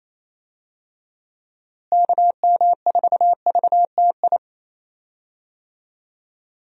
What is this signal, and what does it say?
Morse code, heard as a keyed tone, about 28 wpm, 700 Hz. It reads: KM4VTI
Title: USA Call Signs - 6 Characters - Repeated 2x - 28wpm